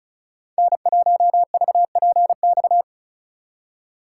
Morse code, heard as a keyed tone, about 47 wpm, spaced as for 41 wpm, 700 Hz. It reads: N1VPX
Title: Speed Racing - USA 1x3 Call Signs 35wpm